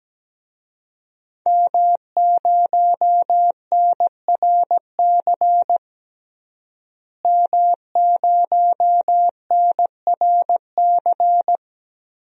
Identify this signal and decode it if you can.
Morse code, heard as a keyed tone, about 17 wpm, 700 Hz. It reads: M0NRC M0NRC
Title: DXCC-100 Call Signs - 5 Characters - Repeated 3x - 17wpm